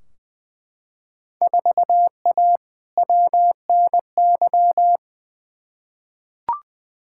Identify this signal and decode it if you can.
Morse code, heard as a keyed tone, about 20 wpm, 700 Hz. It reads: 4A WNY
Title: Field Day Exchanges Encore - 20wpm